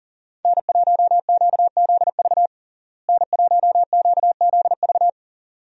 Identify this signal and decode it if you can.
Morse code, heard as a keyed tone, about 40 wpm, 700 Hz. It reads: N1QZV N1QZV
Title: USA Call Signs - 5 Characters - Repeated 2x - 40wpm